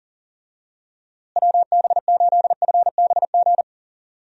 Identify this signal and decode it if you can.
Morse code, heard as a keyed tone, about 40 wpm, 700 Hz. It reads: WB8FBG
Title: USA Call Signs - 6 Characters - Repeated 2x - 40wpm